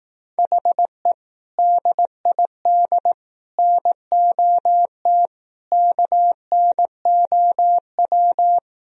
Morse code, heard as a keyed tone, about 18 wpm, 700 Hz. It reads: HE DID NOT KNOW
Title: Speed-Racing - Sets of 4 Words 15wpm